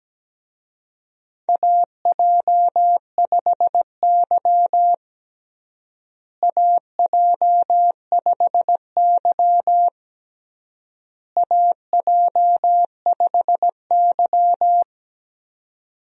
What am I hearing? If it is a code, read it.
Morse code, heard as a keyed tone, about 17 wpm, 700 Hz. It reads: AJ5Y AJ5Y AJ5Y
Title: USA Call Signs - 4 Characters - Repeated 3x - 17wpm